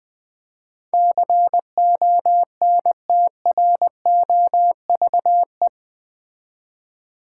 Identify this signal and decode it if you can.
Morse code, heard as a keyed tone, about 20 wpm, 700 Hz. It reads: CONTROVE
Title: Build A Word - 4 Letters At A Time - 11-Letter Words - 20wpm